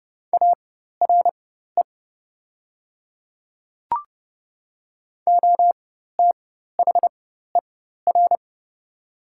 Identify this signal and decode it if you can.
Morse code, heard as a keyed tone, about 30 wpm, spaced as for 14 wpm, 700 Hz. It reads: ARE OTHER
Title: 4x Character Spacing - Top 300 Words - 30wpm